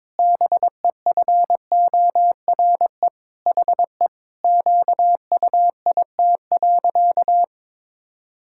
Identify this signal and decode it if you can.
Morse code, heard as a keyed tone, about 22 wpm, 700 Hz. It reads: BEFORE HE QUIT.
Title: Pangram Challenge v2 - 22wpm